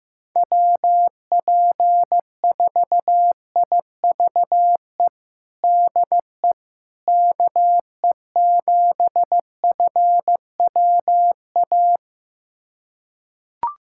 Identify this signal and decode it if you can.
Morse code, heard as a keyed tone, about 15 wpm, 700 Hz. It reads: WP4IVE DE KE7FWA
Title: USA Callsign DE Callsign 15wpm